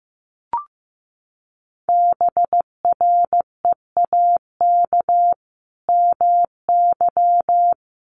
Morse code, heard as a keyed tone, about 15 wpm, 700 Hz. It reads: BREAK MY
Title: Sentences from Top 1400 Words 15wpm